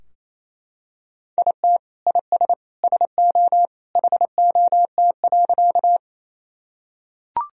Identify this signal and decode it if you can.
Morse code, heard as a keyed tone, about 28 wpm, 700 Hz. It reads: IT IS SO HOT.
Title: Sentences from Top 1000 Words - Review Part 3 - 28wpm